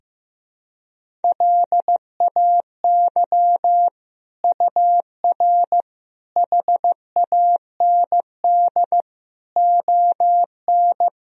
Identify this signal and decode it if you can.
Morse code, heard as a keyed tone, about 15 wpm, 700 Hz. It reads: LAY UR HAND ON
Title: Sentences from Top 400 Words and Abbreviations 15wpm